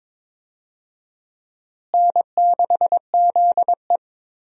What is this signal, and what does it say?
Morse code, heard as a keyed tone, about 22 wpm, 700 Hz. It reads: N6ZE
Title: USA Call Signs - 4 Characters - Repeated 3x - 22wpm